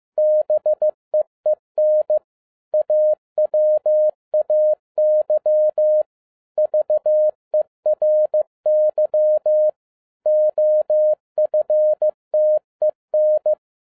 Morse code, heard as a keyed tone, about 15 wpm, 610 Hz. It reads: BEEN AWAY VERY OFTEN